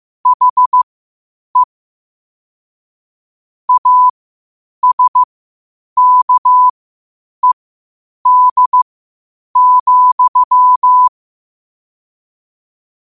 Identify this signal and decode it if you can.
Morse code, heard as a keyed tone, about 15 wpm, spaced as for 8 wpm, 1000 Hz. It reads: HE ASKED,
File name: qotdfiles / 8wpm